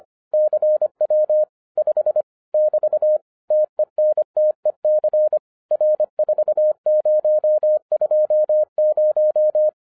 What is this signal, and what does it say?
Morse code, heard as a keyed tone, about 24 wpm, 610 Hz. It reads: CW 5 <BT> TENTEC R4020
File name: 2011-07-mp3 / n24 QRP OTAKU 25wpm